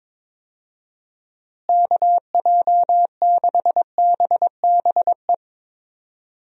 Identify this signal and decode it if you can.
Morse code, heard as a keyed tone, about 22 wpm, 700 Hz. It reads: KJ6BBE
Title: USA Call Signs - 6 Characters - Repeated 3x - 22wpm